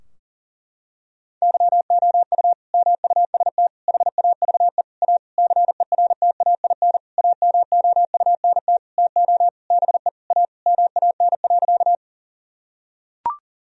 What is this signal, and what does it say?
Morse code, heard as a keyed tone, about 40 wpm, 700 Hz. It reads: YOU MUST HAVE A CERTAIN AMOUNT TO BE A MAN.
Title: Sentences from Top 1000 Words - Review Part 4 - 40wpm